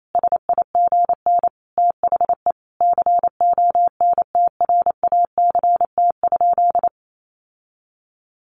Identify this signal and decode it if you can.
Morse code, heard as a keyed tone, about 28 wpm, 700 Hz. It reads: SIGN THE CONTRACT?
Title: Sentences from Top 1400 Words 28wpm